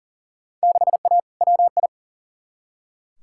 Morse code, heard as a keyed tone, about 40 wpm, 700 Hz. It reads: 6A WI
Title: Field Day Exchanges - 40wpm